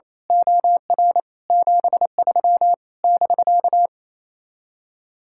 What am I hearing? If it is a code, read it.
Morse code, heard as a keyed tone, about 26 wpm, 700 Hz. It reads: OR 73 <BK>
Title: Speed-Racing Common QSO Phrases - 28wpm